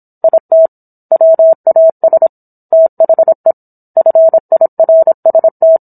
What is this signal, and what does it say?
Morse code, heard as a keyed tone, about 27 wpm, 650 Hz. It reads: IT WAS THE FIRST